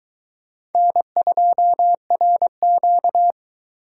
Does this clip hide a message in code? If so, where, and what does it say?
Yes: Morse code from 0.8 to 3.3 s, heard as a keyed tone, about 22 wpm, 700 Hz. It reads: N2RQ